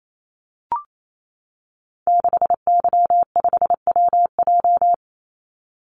Morse code, heard as a keyed tone, about 28 wpm, 700 Hz. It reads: 6Y5WJ
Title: Call Signs - 28wpm